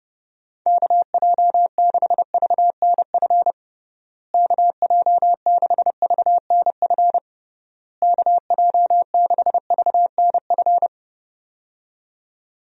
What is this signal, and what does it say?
Morse code, heard as a keyed tone, about 30 wpm, 700 Hz. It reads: KJ6VNF KJ6VNF KJ6VNF
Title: USA Call Signs - 6 Characters - Repeated 3x - 30wpm